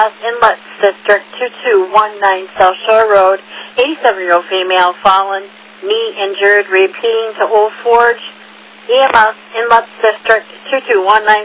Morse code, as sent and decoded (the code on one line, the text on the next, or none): none